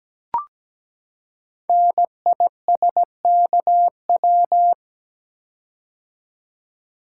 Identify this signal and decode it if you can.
Morse code, heard as a keyed tone, about 17 wpm, 700 Hz. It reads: NISKW
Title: Code Groups - 5 Letters - 17wpm